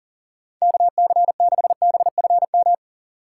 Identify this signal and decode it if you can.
Morse code, heard as a keyed tone, about 40 wpm, 700 Hz. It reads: KC6BFM
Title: USA Call Signs - 6 Characters - 6x Character Spacing - 40wpm